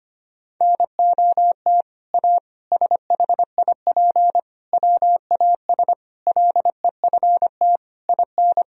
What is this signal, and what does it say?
Morse code, heard as a keyed tone, about 25 wpm, 700 Hz. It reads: NOT A SHIP WAS LEFT IN
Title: Sentences from Top 1000 Words - Review Part 2 - 25wpm